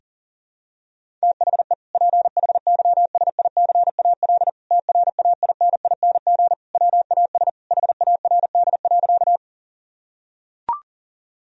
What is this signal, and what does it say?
Morse code, heard as a keyed tone, about 40 wpm, 700 Hz. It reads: THE PHYSICAL TRAINING WAS HARD.
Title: Sentences from Top 1000 Words - Review Part 4 - 40wpm